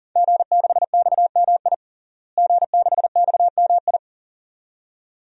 Morse code, heard as a keyed tone, about 40 wpm, 700 Hz. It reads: G6XMI G6XMI
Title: DXCC-100 Call Signs - 5 Characters - Repeated 2x - 40wpm